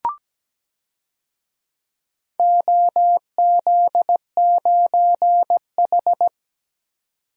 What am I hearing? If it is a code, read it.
Morse code, heard as a keyed tone, about 17 wpm, 700 Hz. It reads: OZ9H
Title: DXCC-100 Call Signs - 4 Characters - 17wpm